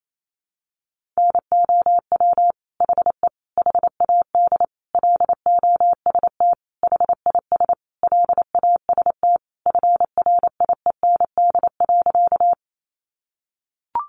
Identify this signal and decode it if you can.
Morse code, heard as a keyed tone, about 28 wpm, 700 Hz. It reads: NOW HE HAD LOST HIS LAST FRIEND.
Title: Sentences from Top 1400 Words 28wpm